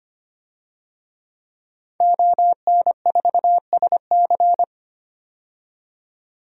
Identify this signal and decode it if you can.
Morse code, heard as a keyed tone, about 25 wpm, 700 Hz. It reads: ON4SC
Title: DXCC-100 Call Signs - 5 Characters - 25wpm